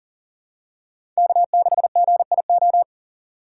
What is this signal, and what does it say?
Morse code, heard as a keyed tone, about 38 wpm, 700 Hz. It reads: K6GIO
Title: USA Call Signs - 5 Characters - 3x Character Spacing - 40wpm